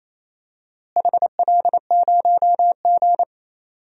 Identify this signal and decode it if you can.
Morse code, heard as a keyed tone, about 28 wpm, 700 Hz. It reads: HL0G
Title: DXCC-100 Call Signs - 4 Characters - Repeated 3x - 28wpm